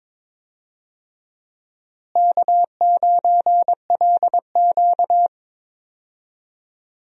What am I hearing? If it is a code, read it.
Morse code, heard as a keyed tone, about 22 wpm, 700 Hz. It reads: K9LQ
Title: USA Call Signs - 4 Characters - Repeated 3x - 22wpm